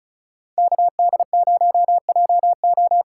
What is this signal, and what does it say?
Morse code, heard as a keyed tone, about 36 wpm, 700 Hz. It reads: KD0JO
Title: USA Call Signs - 6 Characters - 8x Character Spacing - 35wpm